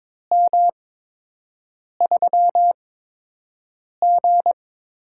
Morse code, heard as a keyed tone, about 22 wpm, spaced as for 6 wpm, 700 Hz. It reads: M3G
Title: USA Call Signs - 4 Characters - 8x Character Spacing - 22wpm